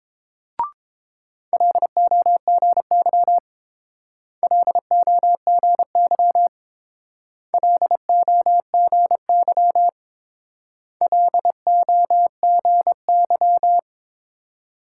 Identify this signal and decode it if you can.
Morse code, heard as a keyed tone, about 27 wpm, 700 Hz. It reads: LOGY LOGY LOGY LOGY
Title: Speed-Racing Multi-letter Phonemes - 22wpm